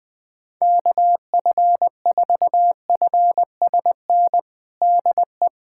Morse code, heard as a keyed tone, about 20 wpm, 700 Hz. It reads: KF4FSN DE
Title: USA Callsign DE Callsign - 6 Characters - 20wpm